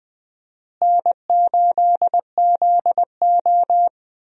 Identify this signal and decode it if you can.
Morse code, heard as a keyed tone, about 20 wpm, 700 Hz. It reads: N8ZO
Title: USA Call Signs - 4 Characters - Repeated 2x - 20wpm